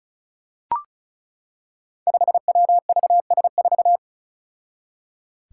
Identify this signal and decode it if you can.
Morse code, heard as a keyed tone, about 34 wpm, 700 Hz. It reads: 5WVS4
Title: Five Challenging Letter-Number Combinations 35wpm